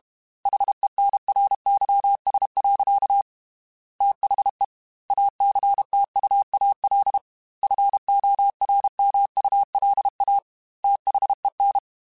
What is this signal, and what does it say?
Morse code, heard as a keyed tone, about 32 wpm, 800 Hz. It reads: HENRYS. THE ACTUAL FORMULA THEN
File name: Technical Subjects / Technical Subjects-mp3s / XL @30